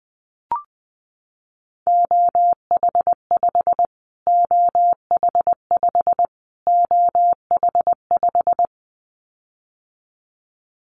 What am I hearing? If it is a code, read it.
Morse code, heard as a keyed tone, about 20 wpm, 700 Hz. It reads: OH5 OH5 OH5
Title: Callsign Prefixes for Top 25 DXCC - Repeated 3x - 20wpm